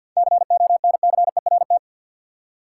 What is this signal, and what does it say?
Morse code, heard as a keyed tone, about 53 wpm, 700 Hz. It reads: CONCERT
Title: Seven Letter Words - 50wpm